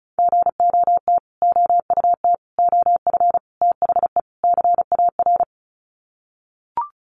Morse code, heard as a keyed tone, about 34 wpm, 700 Hz. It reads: GOT OUT OF THE CAR
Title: Sets of 5 Words Encore 35wpm